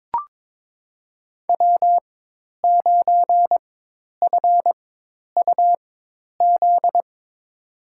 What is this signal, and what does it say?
Morse code, heard as a keyed tone, about 22 wpm, spaced as for 10 wpm, 700 Hz. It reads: W9FUZ
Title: USA Call Signs - 5 Characters - 4x Character Spacing - 22wpm